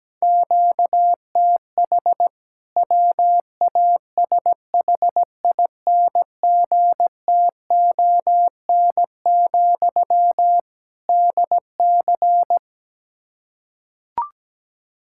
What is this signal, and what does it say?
Morse code, heard as a keyed tone, about 17 wpm, 700 Hz. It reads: QTH WASHINGTON, DC
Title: K1USN SST QSO Practice 17wpm